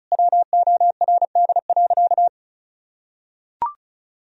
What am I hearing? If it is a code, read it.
Morse code, heard as a keyed tone, about 36 wpm, 700 Hz. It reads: WORD.
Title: Sentences from Top 1000 Words - Review Part 2 - 35wpm